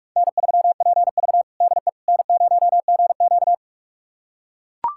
Morse code, heard as a keyed tone, about 43 wpm, 700 Hz. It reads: N3PV DE N0GQ
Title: USA Callsign DE Callsign - 4 Characters - 45wpm